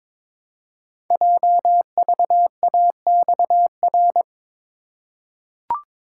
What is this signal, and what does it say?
Morse code, heard as a keyed tone, about 22 wpm, 700 Hz. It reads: JVAXR
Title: Code Groups - 5 Letter:Numbers - 22wpm